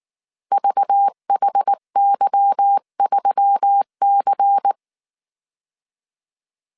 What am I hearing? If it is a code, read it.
Morse code, heard as a keyed tone, about 19 wpm, 800 Hz. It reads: VHY3C
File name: CW / DET / 1 9 1500C S C 18wpm 2539